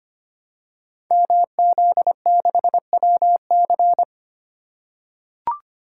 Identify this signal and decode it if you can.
Morse code, heard as a keyed tone, about 25 wpm, 700 Hz. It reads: MZ6WC
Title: Code Groups - 5 Letter:Numbers - 25wpm